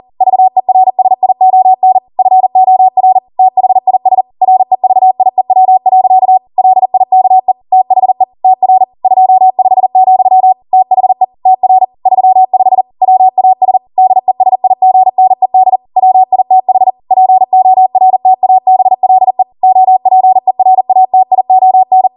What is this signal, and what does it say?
Morse code, heard as a keyed tone, about 40 wpm, 750 Hz. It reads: VERSION FOR THIS REVIEW. LIKE THE TR 25, THE TR 35 WAS DESIGNED WITH PORTABLE OPERATION
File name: Morse / 40WPM